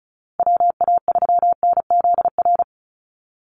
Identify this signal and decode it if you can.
Morse code, heard as a keyed tone, about 34 wpm, 700 Hz. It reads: WA3NZR
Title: Call Signs Repeated 3x - 35wpm